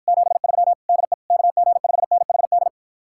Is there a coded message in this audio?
Yes: Morse code from 0.1 to 2.7 s, heard as a keyed tone, about 54 wpm, 700 Hz. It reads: 73 DE KG5NHD